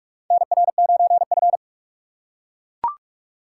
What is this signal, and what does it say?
Morse code, heard as a keyed tone, about 44 wpm, 700 Hz. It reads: NR9F
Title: USA Call Signs - 4 Characters - 4x Character Spacing - 45wpm